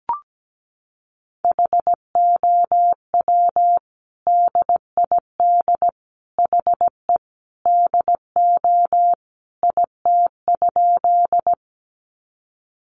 Morse code, heard as a keyed tone, about 17 wpm, 700 Hz. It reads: HOW DID HE DO IT?
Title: Sentences from Top 100 Words 17wpm